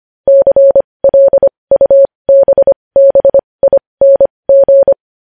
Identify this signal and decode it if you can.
Morse code, heard as a keyed tone, about 25 wpm, spaced as for 20 wpm, 550 Hz. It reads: CLUBBING
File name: podcast / News headlines-1x-25x20